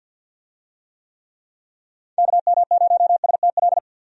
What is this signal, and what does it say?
Morse code, heard as a keyed tone, about 48 wpm, 700 Hz. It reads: KM0STL
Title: USA Call Signs - 6 Characters - 50wpm